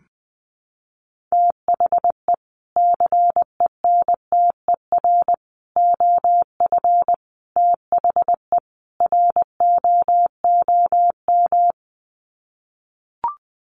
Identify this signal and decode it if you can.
Morse code, heard as a keyed tone, about 20 wpm, 700 Hz. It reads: THE CENTER OF THE ROOM
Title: Sets of 5 Words Encore 20wpm